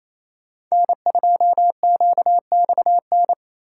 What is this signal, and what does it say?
Morse code, heard as a keyed tone, about 28 wpm, 700 Hz. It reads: N2QXN
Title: USA Call Signs - 5 Characters - 8x Character Spacing - 28wpm